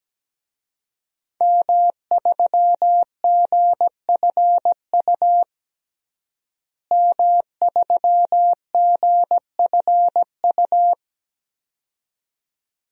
Morse code, heard as a keyed tone, about 17 wpm, 700 Hz. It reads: M3GFU M3GFU
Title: DXCC-100 Call Signs - 5 Characters - Repeated 2x - 17wpm